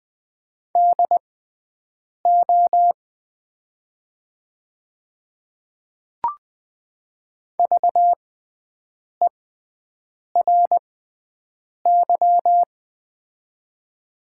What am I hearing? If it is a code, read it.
Morse code, heard as a keyed tone, about 20 wpm, spaced as for 7 wpm, 700 Hz. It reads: DO VERY
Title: 6x Character Spacing - Top 500 Words - 20wpm